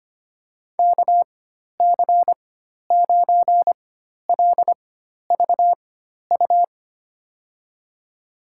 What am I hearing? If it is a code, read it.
Morse code, heard as a keyed tone, about 25 wpm, spaced as for 12 wpm, 700 Hz. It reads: KC9LVU
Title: USA Call Signs - 6 Characters - 4x Character Spacing - 25wpm